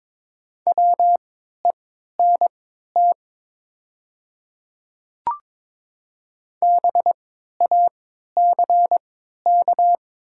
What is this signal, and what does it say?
Morse code, heard as a keyed tone, about 22 wpm, spaced as for 12 wpm, 700 Hz. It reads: WENT BACK
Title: 3x Character Spacing - Top 200 Words - 22wpm